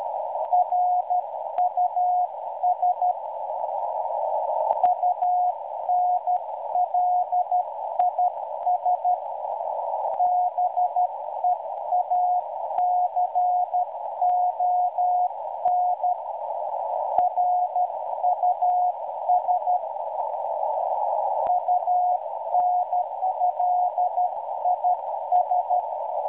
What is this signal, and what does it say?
Morse code, heard as a keyed tone, about 12 wpm, 720 Hz, with noise. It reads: RUS UNLIS BEACON RUS UNLIS